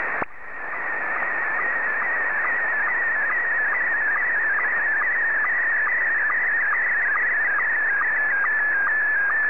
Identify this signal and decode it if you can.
SSTV (slow-scan television), Scottie 1.
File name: sstv / S1-VK3DFL